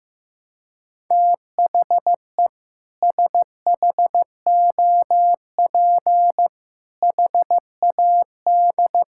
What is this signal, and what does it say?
Morse code, heard as a keyed tone, about 15 wpm, 700 Hz. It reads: THE SHOP HAD
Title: Sentences from Top 1600 Words 15wpm